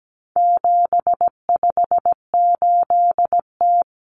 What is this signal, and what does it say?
Morse code, heard as a keyed tone, about 17 wpm, 700 Hz. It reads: 758T
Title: Common Cut Numbers - 4-Digit 17wpm